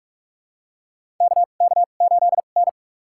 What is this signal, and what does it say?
Morse code, heard as a keyed tone, about 47 wpm, spaced as for 33 wpm, 700 Hz. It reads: KK8N
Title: USA Call Signs - 4 Characters - 2x Character Spacing - 45wpm